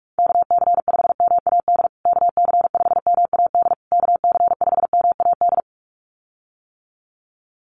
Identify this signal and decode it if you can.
Morse code, heard as a keyed tone, about 44 wpm, 700 Hz. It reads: KC5MAD KC5MAD KC5MAD
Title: USA Call Signs Repeated 3x 45wpm